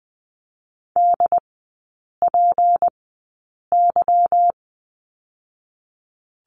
Morse code, heard as a keyed tone, about 20 wpm, spaced as for 8 wpm, 700 Hz. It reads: DPY
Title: 178 Lesson 29e - Slash - Three Characters 20wpm